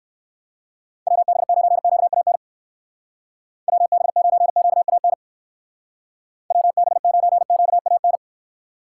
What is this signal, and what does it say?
Morse code, heard as a keyed tone, about 56 wpm, 700 Hz. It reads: WB9QAN WB9QAN WB9QAN
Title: Speed Racing - USA 2x3 Call Signs 45wpm